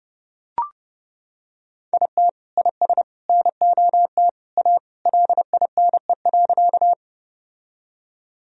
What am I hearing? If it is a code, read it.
Morse code, heard as a keyed tone, about 30 wpm, 700 Hz. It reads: IT IS NOT A LINE.
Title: Sentences from Top 1000 Words - Review Part 1 - 30wpm